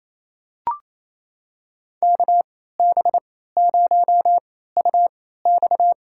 Morse code, heard as a keyed tone, about 28 wpm, spaced as for 16 wpm, 700 Hz. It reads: KB0UX
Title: USA Call Signs - 3x Character Spacing - 28wpm